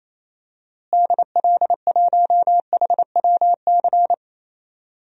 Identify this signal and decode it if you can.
Morse code, heard as a keyed tone, about 28 wpm, 700 Hz. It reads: DL1HWC
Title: DXCC-100 Call Signs - 6 Characters - Repeated 3x - 28wpm